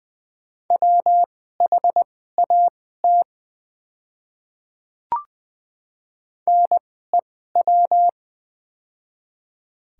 Morse code, heard as a keyed tone, about 21 wpm, spaced as for 15 wpm, 700 Hz. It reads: WHAT NEW
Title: 2x Character Spacing - Top 300 Words - 20wpm